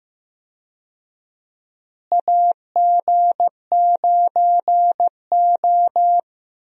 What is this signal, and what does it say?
Morse code, heard as a keyed tone, about 15 wpm, 700 Hz. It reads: AG9O
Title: USA Call Signs - 4 Characters - Repeated 2x - 15wpm